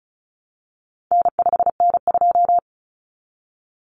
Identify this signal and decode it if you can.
Morse code, heard as a keyed tone, about 34 wpm, 700 Hz. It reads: N5N2
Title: Common Cut Numbers - 4-Digit 35wpm